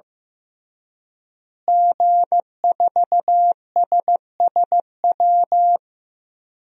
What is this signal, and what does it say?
Morse code, heard as a keyed tone, about 15 wpm, 700 Hz. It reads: G4SSW
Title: DXCC-100 Call Signs - 5 Characters - Repeated 3x - 15wpm